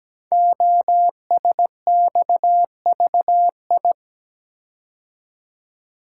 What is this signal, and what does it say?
Morse code, heard as a keyed tone, about 17 wpm, 700 Hz. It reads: OSXVI
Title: Code Groups - 5 Letters - 17wpm